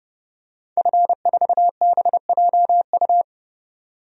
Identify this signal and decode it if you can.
Morse code, heard as a keyed tone, about 30 wpm, 700 Hz. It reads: F4BJU
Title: DXCC-100 Call Signs - 5 Characters - 30wpm